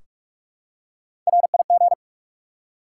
Morse code, heard as a keyed tone, about 46 wpm, 700 Hz. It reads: RIG
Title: Top 90 QSO Elements 45wpm